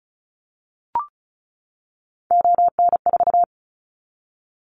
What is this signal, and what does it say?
Morse code, heard as a keyed tone, about 36 wpm, 700 Hz. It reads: ON4
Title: Callsign Prefixes for Top 50 DXCC - 35wpm